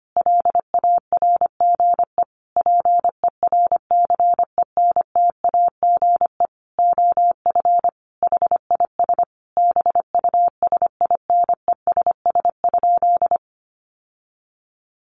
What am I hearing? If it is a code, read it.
Morse code, heard as a keyed tone, about 25 wpm, 700 Hz. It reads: LARGE PERCENTAGE OF HIS BUSINESS?
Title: Sentences from Top 1300 Words 25wpm